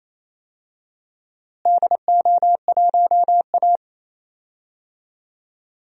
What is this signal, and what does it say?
Morse code, heard as a keyed tone, about 28 wpm, 700 Hz. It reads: DO1A
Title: DXCC-100 Call Signs - 4 Characters - Repeated 3x - 28wpm